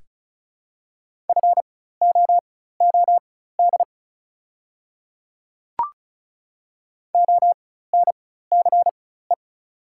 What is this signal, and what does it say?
Morse code, heard as a keyed tone, about 36 wpm, spaced as for 16 wpm, 700 Hz. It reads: FOOD ONCE
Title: 4x Character Spacing - Top 300 Words - 35wpm